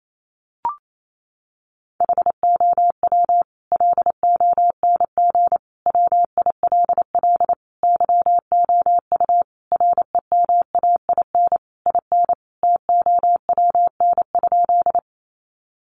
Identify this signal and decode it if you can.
Morse code, heard as a keyed tone, about 28 wpm, 700 Hz. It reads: HOW LONG WILL YOU REMAIN IN TOWN?
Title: Sentences from Top 1200 Words 28wpm